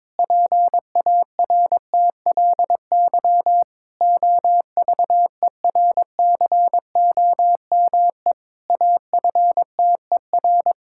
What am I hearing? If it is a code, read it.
Morse code, heard as a keyed tone, about 21 wpm, 700 Hz. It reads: PARTLY OVERCOME AFTER
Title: Pangram Challenge v2 - 22wpm